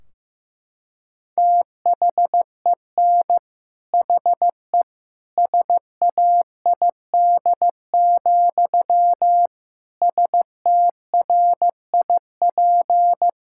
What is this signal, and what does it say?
Morse code, heard as a keyed tone, about 15 wpm, 700 Hz. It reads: THEN HE SAID, STRIP
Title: Sentences from Top 1600 Words 15wpm